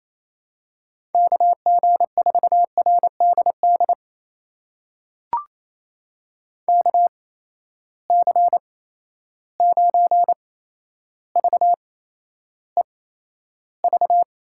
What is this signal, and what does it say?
Morse code, heard as a keyed tone, about 28 wpm, spaced as for 12 wpm, 700 Hz. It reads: KG4RDD KC9VEV
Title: USA Call Signs - 6 Characters - 8x Character Spacing - 28wpm